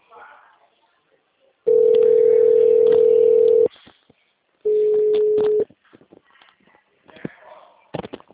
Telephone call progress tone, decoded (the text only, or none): ringback tone